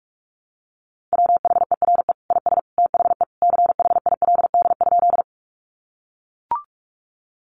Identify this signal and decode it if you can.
Morse code, heard as a keyed tone, about 44 wpm, 700 Hz. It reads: WHERE IS THE CHILD?
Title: Sentences from Top 800 Words 45wpm